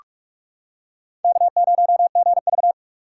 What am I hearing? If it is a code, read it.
Morse code, heard as a keyed tone, about 44 wpm, 700 Hz. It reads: K0GV